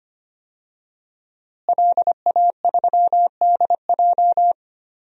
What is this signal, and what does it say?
Morse code, heard as a keyed tone, about 25 wpm, 700 Hz. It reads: LA3DJ
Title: DXCC-100 Call Signs - 5 Characters - Repeated 2x - 25wpm